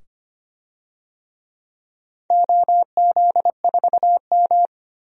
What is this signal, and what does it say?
Morse code, heard as a keyed tone, about 25 wpm, 700 Hz. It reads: OZ4M
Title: DXCC-100 Call Signs - 4 Characters - 25wpm